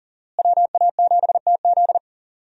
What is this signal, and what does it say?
Morse code, heard as a keyed tone, about 40 wpm, 700 Hz. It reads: WA7TZ
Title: USA Call Signs - 5 Characters - Repeated 2x - 40wpm